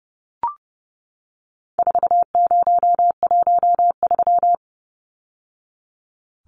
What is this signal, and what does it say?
Morse code, heard as a keyed tone, about 30 wpm, 700 Hz. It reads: 4013
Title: Numbers - 4-Digit 30wpm